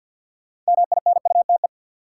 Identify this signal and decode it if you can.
Morse code, heard as a keyed tone, about 48 wpm, 700 Hz. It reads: MINUTE
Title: Sentences from Top 500 Words - New Word Drill - 50wpm